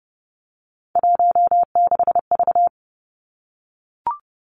Morse code, heard as a keyed tone, about 30 wpm, 700 Hz. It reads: 16V
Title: Three Challenging Letter-Number Combinations 30wpm